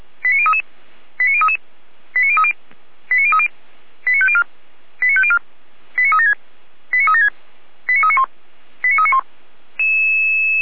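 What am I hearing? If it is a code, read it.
Five-tone selective call (ZVEI), heard as quick runs of five tones, tones of about 70 ms. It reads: 89033 89033 89030 89030 89504 89504 89376 89376 89391 89391
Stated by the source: ZVEI five-tone selective call 89062, received via SDR on channel 455.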